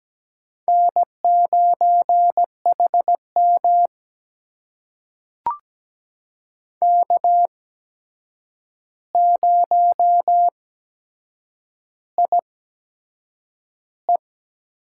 Morse code, heard as a keyed tone, about 17 wpm, spaced as for 7 wpm, 700 Hz. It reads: N9HM K0IE
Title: USA Call Signs - 4 Characters - 8x Character Spacing - 17wpm